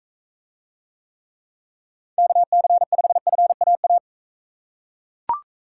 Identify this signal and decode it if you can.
Morse code, heard as a keyed tone, about 39 wpm, 700 Hz. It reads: KC5FAA